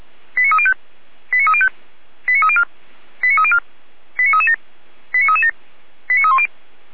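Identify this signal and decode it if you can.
Five-tone selective call (ZVEI), heard as quick runs of five tones, tones of about 70 ms. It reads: 89305 89305 89304 89304 89337 89337 89310
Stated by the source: ZVEI five-tone selective call received by SDR, recorded by Feuerwehr Orlamünde.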